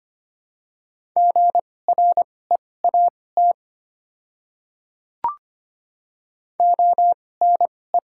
Morse code, heard as a keyed tone, about 25 wpm, spaced as for 18 wpm, 700 Hz. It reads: GREAT ONE
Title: 2x Character Spacing - Top 300 Words - 25wpm